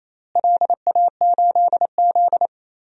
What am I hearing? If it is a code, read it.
Morse code, heard as a keyed tone, about 28 wpm, 700 Hz. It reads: LA8Z
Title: DXCC-100 Call Signs - 4 Characters - Repeated 2x - 28wpm